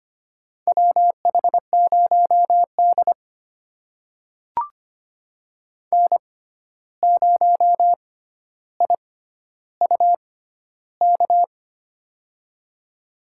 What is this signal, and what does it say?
Morse code, heard as a keyed tone, about 25 wpm, spaced as for 12 wpm, 700 Hz. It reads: WH0D N0IUK
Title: USA Call Signs - 6x Character Spacing - 25wpm